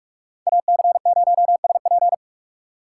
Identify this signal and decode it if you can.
Morse code, heard as a keyed tone, about 46 wpm, 700 Hz. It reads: AC0SP
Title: USA Call Signs - 4x Character Spacing - 45wpm